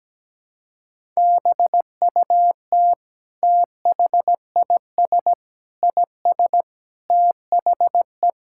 Morse code, heard as a keyed tone, about 17 wpm, 700 Hz. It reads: BUT THIS IS THE
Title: Sentences from Top 1000 Words - Review Part 5 - 17wpm